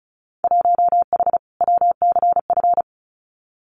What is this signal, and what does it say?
Morse code, heard as a keyed tone, about 36 wpm, 700 Hz. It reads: 1H WCF
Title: Winter Field Day Exchanges 35wpm